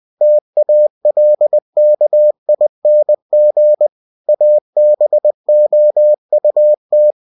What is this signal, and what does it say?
Morse code, heard as a keyed tone, about 20 wpm, 600 Hz. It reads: TALKING ABOUT
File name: cw-books / winnie-the-pooh / winnie-20wpm-page-0030